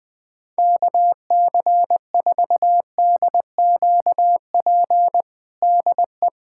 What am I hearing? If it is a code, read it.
Morse code, heard as a keyed tone, about 20 wpm, 700 Hz. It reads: KC4DQP DE
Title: USA Callsign DE Callsign 20wpm